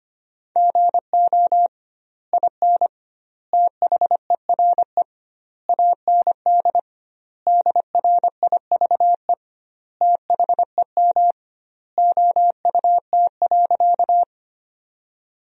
Morse code, heard as a keyed tone, about 25 wpm, 700 Hz. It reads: GO IN THERE AND DRIVE THEM OUT.
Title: 2x Word-Spacing - Sentences from Top 500 Words - 25wpm